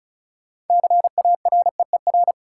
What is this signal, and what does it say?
Morse code, heard as a keyed tone, about 36 wpm, 700 Hz. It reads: CAREER
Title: Sentences from Top 1000 Words - New Word Drill - 35wpm